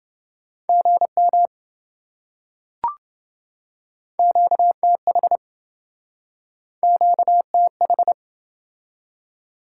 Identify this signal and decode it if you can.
Morse code, heard as a keyed tone, about 27 wpm, 700 Hz. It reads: GM QTH QTH